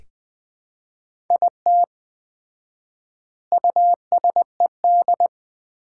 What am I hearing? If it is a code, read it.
Morse code, heard as a keyed tone, about 20 wpm, 700 Hz. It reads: IT USED
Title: 4x Word-Spacing - Sets of 4 Words - 20wpm